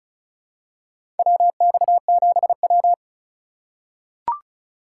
Morse code, heard as a keyed tone, about 34 wpm, 700 Hz. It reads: WX7W
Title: USA Call Signs - 4 Characters - 2x Character Spacing - 35wpm